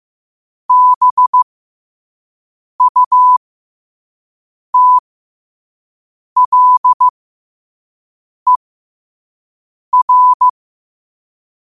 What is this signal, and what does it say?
Morse code, heard as a keyed tone, about 15 wpm, spaced as for 5 wpm, 1000 Hz. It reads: BUTLER
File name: qotdfiles / 5wpm